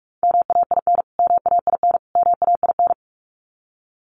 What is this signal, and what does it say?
Morse code, heard as a keyed tone, about 44 wpm, 700 Hz. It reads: MAIN MAIN MAIN
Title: Top 500 Words Repeated 45wpm